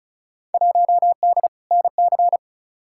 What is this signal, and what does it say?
Morse code, heard as a keyed tone, about 34 wpm, 700 Hz. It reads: 1D NC
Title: Field Day Exchanges Encore - 35wpm